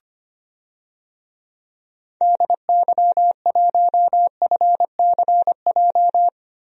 Morse code, heard as a keyed tone, about 25 wpm, 700 Hz. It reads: DY1FCJ
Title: DXCC-100 Call Signs - 6 Characters - Repeated 3x - 25wpm